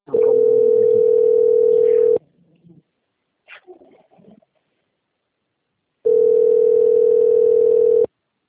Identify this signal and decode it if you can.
Ringback tone (North America), 440 + 480 Hz.
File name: out- - - - - .22504